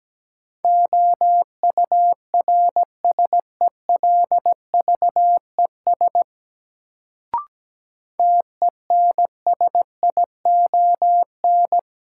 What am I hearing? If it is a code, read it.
Morse code, heard as a keyed tone, about 17 wpm, 700 Hz. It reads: OURSELVES TENSION
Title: Sentences from Top 1600 Words - New Word Drill - 17wpm